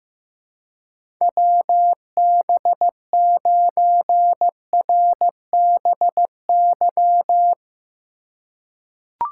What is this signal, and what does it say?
Morse code, heard as a keyed tone, about 15 wpm, 700 Hz. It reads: WB9RBY
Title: USA Call Signs - 3x Character Spacing - 15wpm